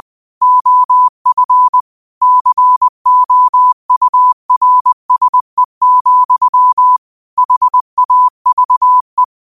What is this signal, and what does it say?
Morse code, heard as a keyed tone, about 20 wpm, 1000 Hz. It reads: OF COURSE, HAVE
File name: qotdfiles / 20wpm